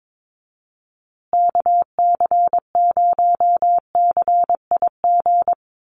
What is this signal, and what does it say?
Morse code, heard as a keyed tone, about 22 wpm, 700 Hz. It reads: KC0CIG
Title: USA Call Signs Repeated 2x 22wpm